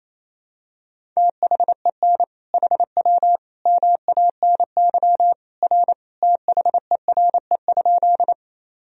Morse code, heard as a keyed tone, about 28 wpm, 700 Hz. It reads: THEN HW MANY R THERE?
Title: Sentences from Top 100 Words and Abbreviations 28wpm